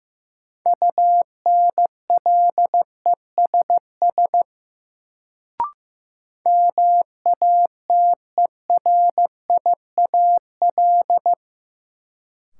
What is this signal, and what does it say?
Morse code, heard as a keyed tone, about 15 wpm, 700 Hz. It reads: UNLESS MATERIAL
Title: Sentences from Top 1000 Words - Review Part 5 - New Word Drill - 15wpm